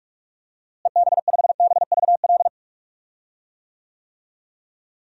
Morse code, heard as a keyed tone, about 46 wpm, 700 Hz. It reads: EB5BVL
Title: DXCC-100 Call Signs - 6 Characters - Repeated 2x - 45wpm